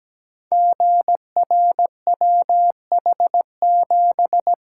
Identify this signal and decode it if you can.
Morse code, heard as a keyed tone, about 17 wpm, 700 Hz. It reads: GRWH7
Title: Code Groups - 5 Letter:Numbers - 17wpm